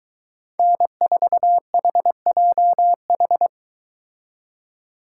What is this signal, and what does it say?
Morse code, heard as a keyed tone, about 22 wpm, 700 Hz. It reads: N4HJH